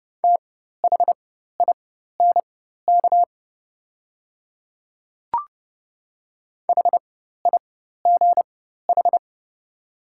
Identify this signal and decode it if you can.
Morse code, heard as a keyed tone, about 30 wpm, spaced as for 14 wpm, 700 Hz. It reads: THINK HIGH
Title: 4x Character Spacing - Top 200 Words - 30wpm